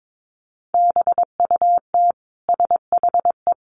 Morse code, heard as a keyed tone, about 22 wpm, 700 Hz. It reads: BUT SHE
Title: Sentences from Top 600 Words 22wpm